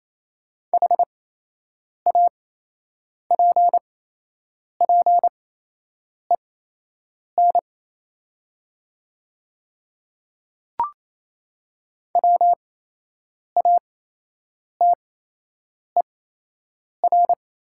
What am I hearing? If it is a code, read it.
Morse code, heard as a keyed tone, about 28 wpm, spaced as for 8 wpm, 700 Hz. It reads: HAPPEN WATER
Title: 8x Character Spacing - Top 500 Words - 28wpm